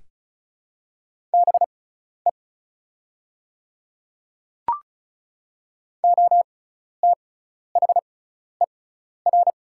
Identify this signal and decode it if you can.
Morse code, heard as a keyed tone, about 36 wpm, spaced as for 12 wpm, 690 Hz. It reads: BE OTHER
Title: 6x Character Spacing - Top 100 Words - 35wpm